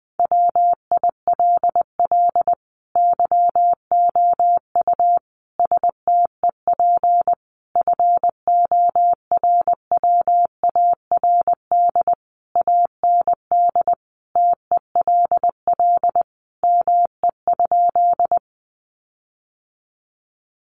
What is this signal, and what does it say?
Morse code, heard as a keyed tone, about 20 wpm, 700 Hz. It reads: WILL YOU STEP FORWARD AND TELL ME?
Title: Sentences from Top 1300 Words 20wpm